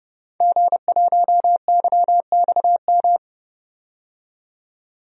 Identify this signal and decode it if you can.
Morse code, heard as a keyed tone, about 31 wpm, 700 Hz. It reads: G1YXM
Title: DXCC-100 Call Signs - 5 Characters - 30wpm